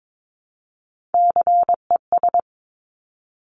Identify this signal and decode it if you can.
Morse code, heard as a keyed tone, about 22 wpm, 700 Hz. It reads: CES
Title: Top Three Letter Combinations in Words 22wpm